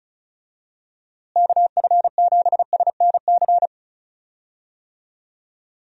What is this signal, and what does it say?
Morse code, heard as a keyed tone, about 34 wpm, 700 Hz. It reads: KF7SNC
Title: USA Call Signs - 6 Characters - Repeated 2x - 35wpm